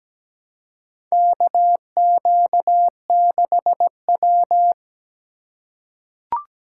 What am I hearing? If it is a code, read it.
Morse code, heard as a keyed tone, about 17 wpm, 700 Hz. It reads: KQ6W
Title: USA Call Signs - 2x Character Spacing - 17wpm